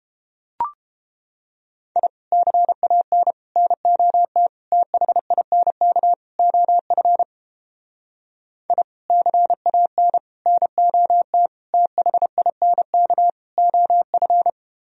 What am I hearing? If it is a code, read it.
Morse code, heard as a keyed tone, about 31 wpm, 700 Hz. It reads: I CAN NOT THINK OF I CAN NOT THINK OF